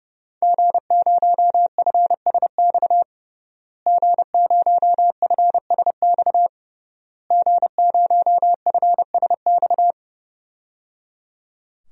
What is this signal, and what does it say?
Morse code, heard as a keyed tone, about 30 wpm, 700 Hz. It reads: G0FSX G0FSX G0FSX
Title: DXCC-100 Call Signs - 5 Characters - Repeated 3x - 30wpm